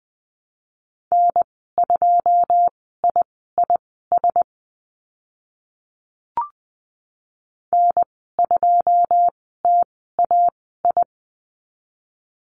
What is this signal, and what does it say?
Morse code, heard as a keyed tone, about 20 wpm, spaced as for 15 wpm, 700 Hz. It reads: N2IIS N2TAI
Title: 088 Lesson 15g - Number 5 - Call Signs 20wpm